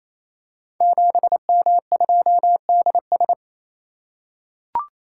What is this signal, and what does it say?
Morse code, heard as a keyed tone, about 27 wpm, 700 Hz. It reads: 7M2DS
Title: Code Groups - 5 Letter:Numbers - 28wpm